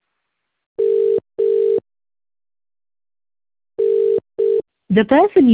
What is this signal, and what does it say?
Ringback tone (UK), 400 + 450 Hz, with speech.